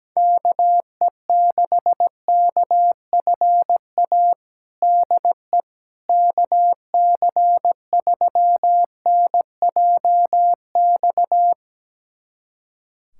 Morse code, heard as a keyed tone, about 17 wpm, 700 Hz. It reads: KE6KFA DE KC3NJX
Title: USA Callsign DE Callsign - 6 Characters - 17wpm